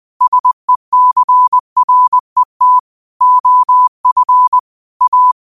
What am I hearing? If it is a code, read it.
Morse code, heard as a keyed tone, about 20 wpm, 1000 Hz. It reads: SECRET OF A